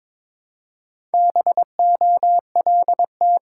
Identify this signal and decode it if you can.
Morse code, heard as a keyed tone, about 22 wpm, 700 Hz. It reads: BOLT
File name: Difficult Words with Hints - 22wpm